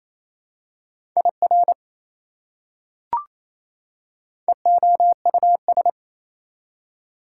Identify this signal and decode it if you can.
Morse code, heard as a keyed tone, about 28 wpm, 700 Hz. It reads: IR EOUS
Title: Prefixes and Suffixes 28wpm